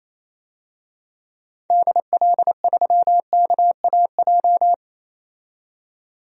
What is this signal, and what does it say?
Morse code, heard as a keyed tone, about 28 wpm, 700 Hz. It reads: DL3KAJ
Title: DXCC-100 Call Signs - 6 Characters - 28wpm